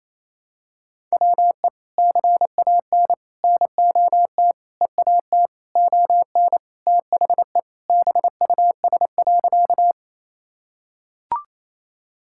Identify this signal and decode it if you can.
Morse code, heard as a keyed tone, about 28 wpm, 700 Hz. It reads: WE CAN NOT EAT ON THE BUS.
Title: Sentences from Top 1000 Words - Review Part 6 - 28wpm